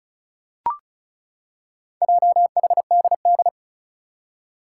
Morse code, heard as a keyed tone, about 36 wpm, 700 Hz. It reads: JHDD
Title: Four Challenging Letter-Number Combinations 35wpm